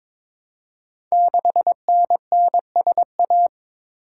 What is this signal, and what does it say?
Morse code, heard as a keyed tone, about 22 wpm, 700 Hz. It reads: 6NNSA
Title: Code Groups - 5 Letter:Numbers - 22wpm